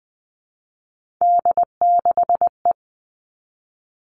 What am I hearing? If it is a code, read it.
Morse code, heard as a keyed tone, about 19 wpm, 700 Hz. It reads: D6E
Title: Three Challenging Letter-Number Combinations 20wpm